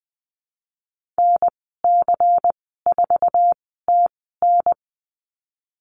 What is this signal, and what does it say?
Morse code, heard as a keyed tone, about 20 wpm, spaced as for 14 wpm, 700 Hz. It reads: NC4TN
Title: 154 Lesson 25g - Letter p - Call Signs 20wpm